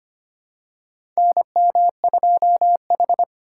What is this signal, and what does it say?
Morse code, heard as a keyed tone, about 25 wpm, 700 Hz. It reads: NM2H
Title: USA Call Signs - 4 Characters - 3x Character Spacing - 25wpm